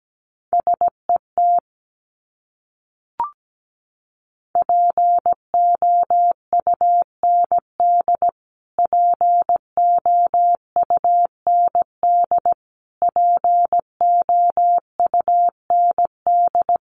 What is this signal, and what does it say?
Morse code, heard as a keyed tone, about 17 wpm, 700 Hz. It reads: SET POUND POUND POUND
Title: Top 500 Words Repeated 17wpm